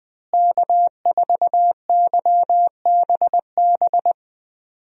Morse code, heard as a keyed tone, about 20 wpm, 700 Hz. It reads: K4YBB
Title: USA Call Signs - 5 Characters - Repeated 2x - 20wpm